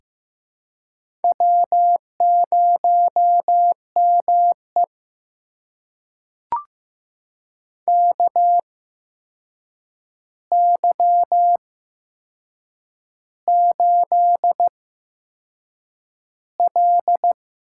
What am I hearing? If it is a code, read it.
Morse code, heard as a keyed tone, about 15 wpm, spaced as for 6 wpm, 700 Hz. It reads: W0ME KY8L
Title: USA Call Signs - 4 Characters - 8x Character Spacing - 15wpm